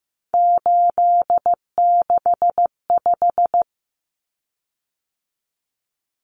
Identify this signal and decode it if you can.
Morse code, heard as a keyed tone, about 15 wpm, 700 Hz. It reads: 865
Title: Common Cut Numbers - 3-Digit 15wpm